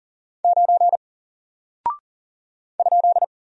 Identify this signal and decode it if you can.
Morse code, heard as a keyed tone, about 40 wpm, 700 Hz. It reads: 9 ?
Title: Single Character (Letter:Number:Punctuation:Prosign) Alt 40wpm